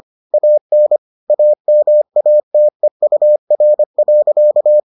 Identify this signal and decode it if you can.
Morse code, heard as a keyed tone, about 25 wpm, 600 Hz. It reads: AN AMATEUR.